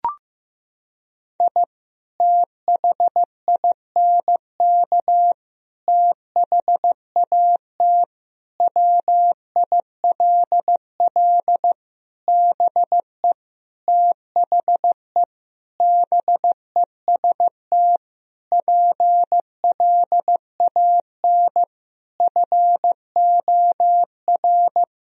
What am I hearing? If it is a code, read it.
Morse code, heard as a keyed tone, about 15 wpm, 700 Hz. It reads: I THINK THAT WILL BE THE BEST PLAN FOR
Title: Sentences from Top 1000 Words - Review Part 2 - 15wpm